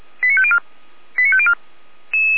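Five-tone selective call (ZVEI), heard as quick runs of five tones, tones of about 70 ms. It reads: 89504 89504
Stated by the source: ZVEI five-tone selective call received on SDR, recorded by Feuerwehr Orlamünde (fire brigade).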